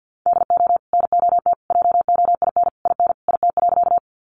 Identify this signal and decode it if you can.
Morse code, heard as a keyed tone, about 50 wpm, 700 Hz. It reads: DO NOT JOIN IN IT.
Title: Sentences from Top 1400 Words 50wpm